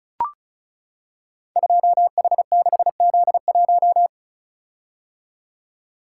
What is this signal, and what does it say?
Morse code, heard as a keyed tone, about 36 wpm, 700 Hz. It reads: 2H6Z1
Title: Code Groups - 5 Letter:Numbers - 35wpm